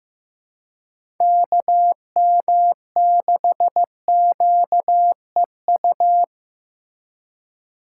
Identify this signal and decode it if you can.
Morse code, heard as a keyed tone, about 15 wpm, 700 Hz. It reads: KM6QEU
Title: USA Call Signs - 6 Characters - 2x Character Spacing - 15wpm